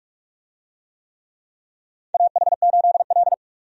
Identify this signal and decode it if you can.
Morse code, heard as a keyed tone, about 47 wpm, 700 Hz. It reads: AH8L
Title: USA Call Signs - 4 Characters - 45wpm